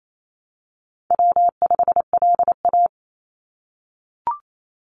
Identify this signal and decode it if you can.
Morse code, heard as a keyed tone, about 28 wpm, 700 Hz. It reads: W5LA
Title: Call Signs - 28wpm